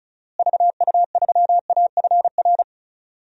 Morse code, heard as a keyed tone, about 36 wpm, 700 Hz. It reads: VU3AFR
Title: DXCC-100 Call Signs - 6 Characters - Repeated 3x - 35wpm